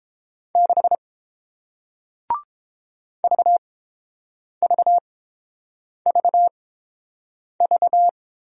Morse code, heard as a keyed tone, about 30 wpm, spaced as for 7 wpm, 700 Hz. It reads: 6 VVVV